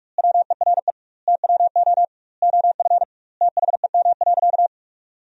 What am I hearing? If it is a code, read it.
Morse code, heard as a keyed tone, about 42 wpm, 700 Hz. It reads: WERE TWO OF THEM.
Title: 2x Word-Spacing - Sentences from Top 100 Words - 45wpm